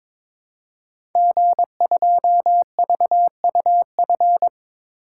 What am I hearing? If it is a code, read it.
Morse code, heard as a keyed tone, about 22 wpm, 700 Hz. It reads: G2VUF
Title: Code Groups - 5 Letter:Numbers - 22wpm